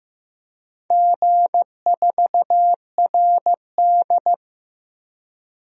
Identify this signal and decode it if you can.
Morse code, heard as a keyed tone, about 15 wpm, 700 Hz. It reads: G4RD
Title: DXCC-100 Call Signs - 4 Characters - 15wpm